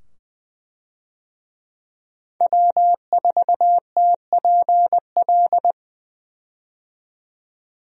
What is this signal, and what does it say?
Morse code, heard as a keyed tone, about 20 wpm, 700 Hz. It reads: W4TPL
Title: USA Call Signs - 5 Characters - 20wpm